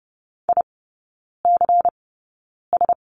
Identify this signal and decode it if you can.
Morse code, heard as a keyed tone, about 30 wpm, spaced as for 9 wpm, 700 Hz. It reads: ICS
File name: Lesson 16e - Letter c - Three Characters 30wpm